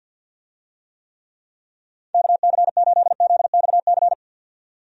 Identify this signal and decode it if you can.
Morse code, heard as a keyed tone, about 51 wpm, 700 Hz. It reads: KC8ZXC
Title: USA Call Signs - 6 Characters - 50wpm